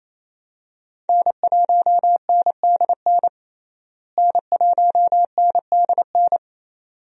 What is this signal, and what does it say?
Morse code, heard as a keyed tone, about 28 wpm, 700 Hz. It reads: N1NDN N1NDN
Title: USA Call Signs - 5 Characters - Repeated 2x - 28wpm